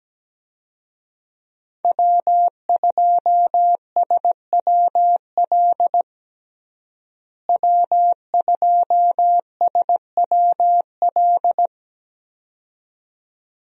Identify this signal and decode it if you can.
Morse code, heard as a keyed tone, about 17 wpm, 700 Hz. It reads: W2SWL W2SWL
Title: USA Call Signs - 5 Characters - Repeated 2x - 17wpm